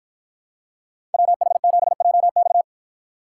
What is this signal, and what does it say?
Morse code, heard as a keyed tone, about 53 wpm, 700 Hz. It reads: WH7JX